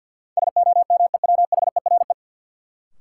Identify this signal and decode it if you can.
Morse code, heard as a keyed tone, about 51 wpm, 700 Hz. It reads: SOMEWHERE
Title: Sentences from Top 1400 Words - New Word Drill Alt - 50wpm